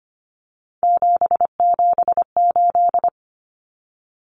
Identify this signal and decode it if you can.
Morse code, heard as a keyed tone, about 25 wpm, 700 Hz. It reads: 778
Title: Numbers - 3-Digit 25wpm